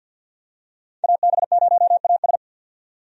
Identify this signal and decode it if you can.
Morse code, heard as a keyed tone, about 53 wpm, 700 Hz. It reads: AB0AS
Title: USA Call Signs - 5 Characters - 50wpm